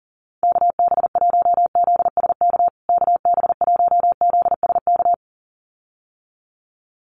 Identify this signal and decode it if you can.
Morse code, heard as a keyed tone, about 40 wpm, 700 Hz. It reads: KB1ZSK KB1ZSK
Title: USA Call Signs Repeated 2x 40wpm